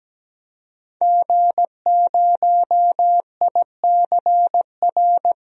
Morse code, heard as a keyed tone, about 17 wpm, 700 Hz. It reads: G0ICR
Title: DXCC-100 Call Signs - 5 Characters - Repeated 3x - 17wpm